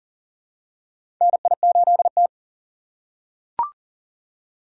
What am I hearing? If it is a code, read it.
Morse code, heard as a keyed tone, about 40 wpm, 700 Hz. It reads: NI8T
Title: USA Call Signs - 6x Character Spacing - 40wpm